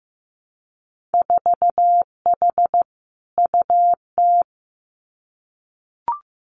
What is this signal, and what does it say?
Morse code, heard as a keyed tone, about 15 wpm, 700 Hz. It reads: 4H UT
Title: Winter Field Day Exchanges Encore 15wpm